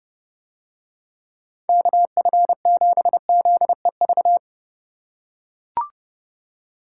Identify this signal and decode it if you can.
Morse code, heard as a keyed tone, about 29 wpm, 700 Hz. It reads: KF7ZEV